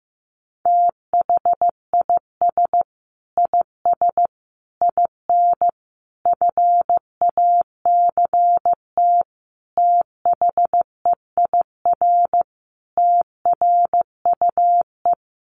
Morse code, heard as a keyed tone, about 15 wpm, 700 Hz. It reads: THIS IS IN FACT THEIR TRUE
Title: Sentences from Top 1200 Words 15wpm